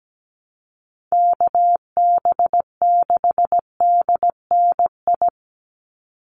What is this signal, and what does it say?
Morse code, heard as a keyed tone, about 17 wpm, 700 Hz. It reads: KB6DNI
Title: USA Call Signs 17wpm